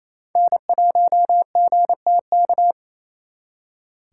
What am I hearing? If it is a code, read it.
Morse code, heard as a keyed tone, about 34 wpm, 700 Hz. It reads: N1GTK
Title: Speed Racing - USA 1x3 Call Signs 28wpm